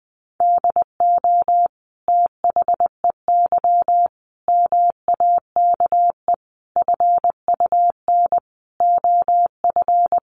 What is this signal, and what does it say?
Morse code, heard as a keyed tone, about 20 wpm, 700 Hz. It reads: DO THEY MAKE FUN OF
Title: Sentences from Top 1300 Words 20wpm